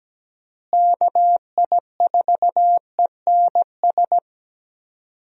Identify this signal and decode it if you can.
Morse code, heard as a keyed tone, about 17 wpm, 700 Hz. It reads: KI4ENS
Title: USA Call Signs - 6 Characters - Repeated 3x - 17wpm